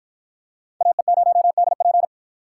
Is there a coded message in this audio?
Yes: Morse code from 0.8 to 2.1 s, heard as a keyed tone, about 52 wpm, 700 Hz. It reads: AE0DP